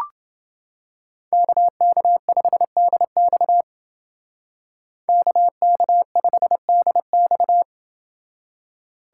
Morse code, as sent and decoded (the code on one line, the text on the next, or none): -.- -.- ..... -.. -..- / -.- -.- ..... -.. -..-
KK5DX KK5DX